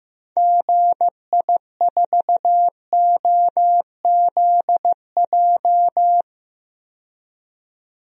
Morse code, heard as a keyed tone, about 15 wpm, 700 Hz. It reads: GI4OZJ
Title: DXCC-100 Call Signs - 6 Characters - 15wpm